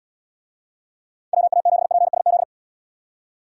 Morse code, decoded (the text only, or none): WA7PSZ